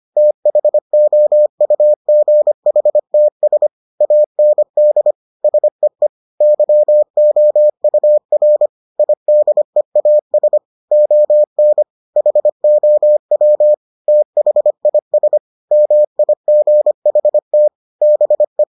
Morse code, heard as a keyed tone, about 25 wpm, 600 Hz. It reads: THOUGHTS AND SEE YOUR IDEAS ON HOW THIS MIGHT BE